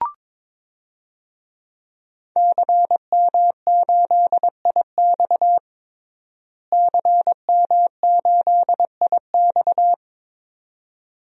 Morse code, heard as a keyed tone, about 22 wpm, 700 Hz. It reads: CM8IX CM8IX
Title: DXCC-100 Call Signs - 5 Characters - Repeated 2x - 22wpm